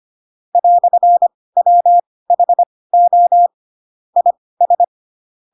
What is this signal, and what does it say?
Morse code, heard as a keyed tone, about 25 wpm, spaced as for 18 wpm, 700 Hz. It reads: "WHO IS
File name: Pinocchio / Pinocchio ch4 12 25 ik1hge